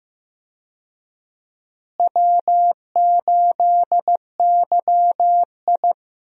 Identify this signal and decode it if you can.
Morse code, heard as a keyed tone, about 15 wpm, 700 Hz. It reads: W8YI
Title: USA Call Signs - 4 Characters - Repeated 3x - 15wpm